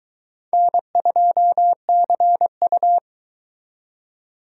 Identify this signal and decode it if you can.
Morse code, heard as a keyed tone, about 22 wpm, 700 Hz. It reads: N2CU